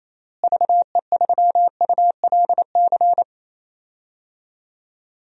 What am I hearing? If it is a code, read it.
Morse code, heard as a keyed tone, about 27 wpm, 700 Hz. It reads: VE3ULC
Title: DXCC-100 Call Signs - 6 Characters - Repeated 2x - 28wpm